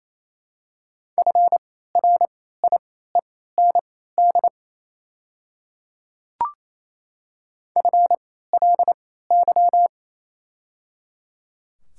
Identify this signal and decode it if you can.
Morse code, heard as a keyed tone, about 28 wpm, spaced as for 16 wpm, 700 Hz. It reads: FRIEND FLY
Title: 3x Character Spacing - Top 500 Words - 28wpm